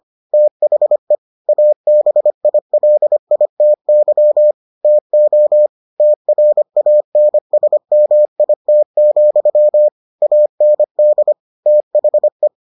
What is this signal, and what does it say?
Morse code, heard as a keyed tone, about 25 wpm, 600 Hz. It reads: THE ABILITY TO TRANSMIT, AND THE